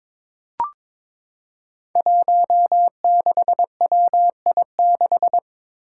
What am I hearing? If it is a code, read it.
Morse code, heard as a keyed tone, about 21 wpm, 700 Hz. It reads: 16WI6
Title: Five Challenging Letter-Number Combinations 22wpm